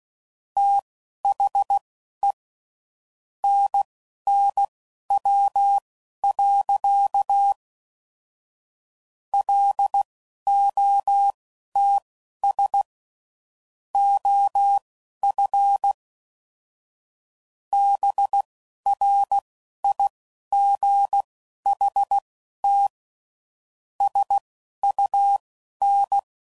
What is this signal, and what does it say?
Morse code, heard as a keyed tone, about 16 wpm, spaced as for 11 wpm, 800 Hz. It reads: THE NNW. LOTS OF BRIGHT SUN
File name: CW Code Groups @10WPM / Weather Sets / Weather 5 12